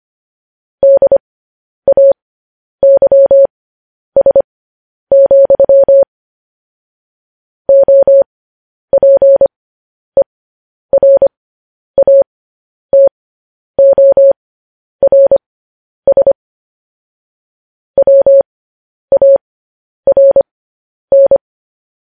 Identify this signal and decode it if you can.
Morse code, heard as a keyed tone, about 25 wpm, spaced as for 10 wpm, 550 Hz. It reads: DAYS, OPERATORS WARN
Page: news-headlines-in-morse-code-at-10-wpm